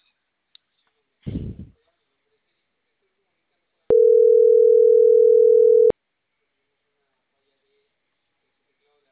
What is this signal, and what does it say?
Ringback tone (North America), 440 + 480 Hz.
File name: CENTRAL CAEA / out agcaea10